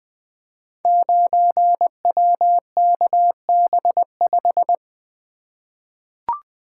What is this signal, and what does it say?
Morse code, heard as a keyed tone, about 20 wpm, 700 Hz. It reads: 9WKB5
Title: Code Groups - 5 Letter:Numbers - 20wpm